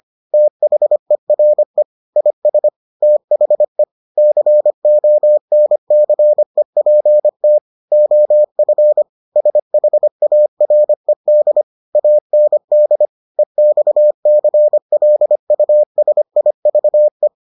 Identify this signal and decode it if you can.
Morse code, heard as a keyed tone, about 25 wpm, 600 Hz. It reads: THERE IS THE CONCEPT OF SHARED AND EXCLUSIVE